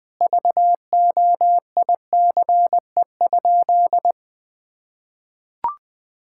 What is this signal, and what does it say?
Morse code, heard as a keyed tone, about 20 wpm, 700 Hz. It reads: VOICE?
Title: Sentences from Top 1000 Words - Review Part 6 - 20wpm